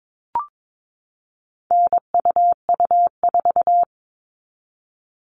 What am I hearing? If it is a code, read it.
Morse code, heard as a keyed tone, about 22 wpm, 700 Hz. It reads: NUU4
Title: All Cut Numbers - 4-Digit 22wpm